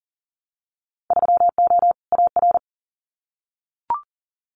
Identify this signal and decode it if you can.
Morse code, heard as a keyed tone, about 40 wpm, 700 Hz. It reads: 3O AR
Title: Winter Field Day Exchanges Encore 40wpm